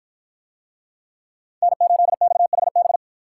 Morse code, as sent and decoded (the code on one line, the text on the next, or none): -. ---.. -..- .... -...
N8XHB